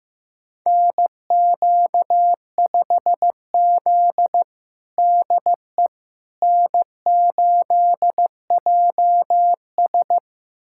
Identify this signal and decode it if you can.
Morse code, heard as a keyed tone, about 15 wpm, 700 Hz. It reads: NQ5Z DE N8JS
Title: USA Callsign DE Callsign - 4 Characters - 15wpm